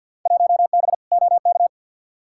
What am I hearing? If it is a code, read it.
Morse code, heard as a keyed tone, about 50 wpm, 700 Hz. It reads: 1B OK
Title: Field Day Exchanges Encore - 50wpm